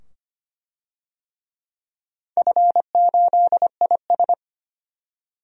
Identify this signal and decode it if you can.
Morse code, heard as a keyed tone, about 25 wpm, 700 Hz. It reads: F8IS
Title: DXCC-100 Call Signs - 4 Characters - 25wpm